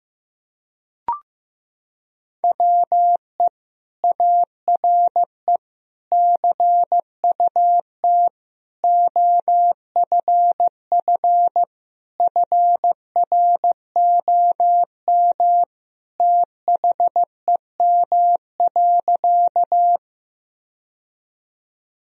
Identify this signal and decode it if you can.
Morse code, heard as a keyed tone, about 15 wpm, 700 Hz. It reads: WE ARE CUT OFF FROM THEM.
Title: Sentences from Top 1000 Words - Review Part 1 - 15wpm